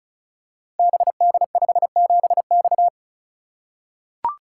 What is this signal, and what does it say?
Morse code, heard as a keyed tone, about 36 wpm, 700 Hz. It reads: BD57X
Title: Code Groups - 5 Letter:Numbers - 35wpm